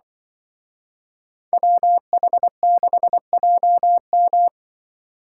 Morse code, heard as a keyed tone, about 24 wpm, 700 Hz. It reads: WH6JM
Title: Speed Racing - USA Call Signs 20wpm